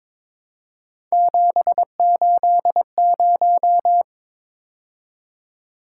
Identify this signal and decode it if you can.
Morse code, heard as a keyed tone, about 27 wpm, 700 Hz. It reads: 780
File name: Speed-Racing - Three Letter-Number Combinations - 22wpm